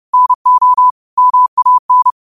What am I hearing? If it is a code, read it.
Morse code, heard as a keyed tone, about 30 wpm, 1000 Hz. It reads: NO MAN
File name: qotdfiles / 30wpm